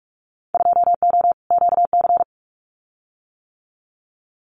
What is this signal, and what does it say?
Morse code, heard as a keyed tone, about 44 wpm, 700 Hz. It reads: 2O QC
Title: Winter Field Day Exchanges 45wpm